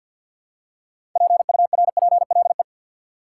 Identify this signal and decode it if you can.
Morse code, heard as a keyed tone, about 50 wpm, 700 Hz. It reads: PURPLE
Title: Six Letter Words - 50wpm